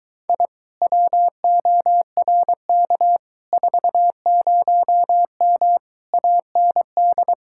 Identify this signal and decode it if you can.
Morse code, heard as a keyed tone, about 23 wpm, 700 Hz. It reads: I WORK 40M AND